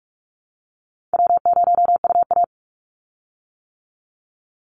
Morse code, heard as a keyed tone, about 44 wpm, 700 Hz. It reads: W0UA
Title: Call Signs - 45wpm